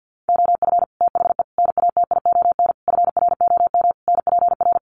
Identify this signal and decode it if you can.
Morse code, heard as a keyed tone, about 50 wpm, 700 Hz. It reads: OF THE NATION FROM NPR
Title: Sets of 5 Words 50wpm